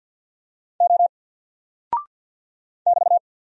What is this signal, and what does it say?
Morse code, heard as a keyed tone, about 48 wpm, 700 Hz. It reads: O <BT>
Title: Single Character (Letter:Number:Punctuation:Prosign) Alt 50wpm